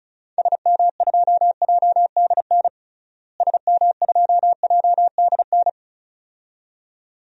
Morse code, heard as a keyed tone, about 36 wpm, 700 Hz. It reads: SM2JDN SM2JDN
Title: DXCC-100 Call Signs - 6 Characters - Repeated 2x - 35wpm